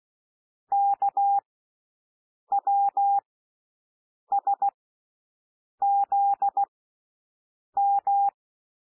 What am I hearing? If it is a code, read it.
Morse code, heard as a keyed tone, about 16 wpm, spaced as for 6 wpm, 800 Hz. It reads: KWSZM
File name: cwmp3 / L1 SOUND LETTERS 16 wpm